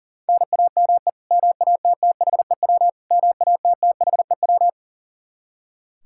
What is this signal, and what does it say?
Morse code, heard as a keyed tone, about 37 wpm, 700 Hz. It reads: NAME MATTHEW MATTHEW
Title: K1USN SST QSO Practice 40wpm